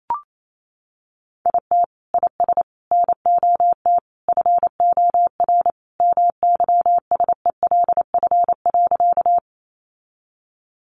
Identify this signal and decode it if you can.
Morse code, heard as a keyed tone, about 28 wpm, 700 Hz. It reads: IT IS NOT FOR MYSELF.
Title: Sentences from Top 1300 Words 28wpm